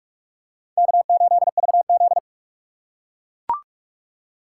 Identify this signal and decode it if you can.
Morse code, heard as a keyed tone, about 44 wpm, 690 Hz. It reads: K8VZ
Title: USA Call Signs - 4 Characters - 8x Character Spacing - 45wpm